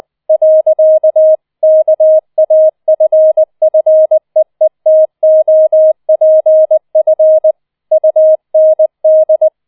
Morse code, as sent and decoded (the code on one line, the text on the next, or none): .-.-.- / -.- .- ..-. ..-. . - --- .--. ..-. / ..- -. -..
. KAFFEETOPF UND